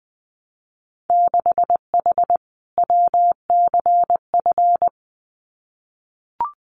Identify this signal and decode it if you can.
Morse code, heard as a keyed tone, about 20 wpm, 700 Hz. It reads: 6H WCF
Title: Winter Field Day Exchanges Encore 20wpm